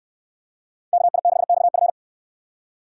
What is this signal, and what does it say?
Morse code, heard as a keyed tone, about 69 wpm, 700 Hz. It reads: KI6XF